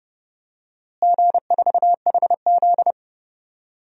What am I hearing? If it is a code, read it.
Morse code, heard as a keyed tone, about 30 wpm, 700 Hz. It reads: G4HZ
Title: DXCC-100 Call Signs - 4 Characters - Repeated 3x - 30wpm